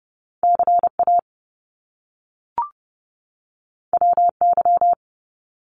Morse code, heard as a keyed tone, about 30 wpm, 700 Hz. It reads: CA WY
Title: US States 30wpm